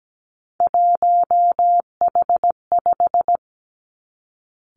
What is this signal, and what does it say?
Morse code, heard as a keyed tone, about 17 wpm, 700 Hz. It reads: 1H5
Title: Three Challenging Letter-Number Combinations 17wpm